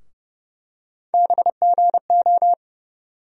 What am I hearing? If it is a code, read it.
Morse code, heard as a keyed tone, about 32 wpm, 700 Hz. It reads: BGO
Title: USA Call Sign Postfixes - 3 Letters 30wpm